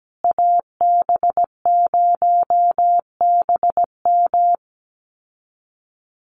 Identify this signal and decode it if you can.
Morse code, heard as a keyed tone, about 17 wpm, 700 Hz. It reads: AB0BM
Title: USA Call Signs Repeated 2x 17wpm